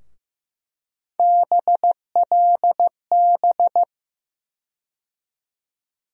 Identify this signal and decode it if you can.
Morse code, heard as a keyed tone, about 15 wpm, 700 Hz. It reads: BLB
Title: USA Call Sign Postfixes - 3 Letters 15wpm